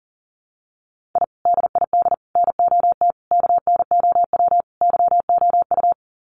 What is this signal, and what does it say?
Morse code, heard as a keyed tone, about 40 wpm, 700 Hz. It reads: I DID NOT KNOW YOU
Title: Sets of 5 Words Encore2 40wpm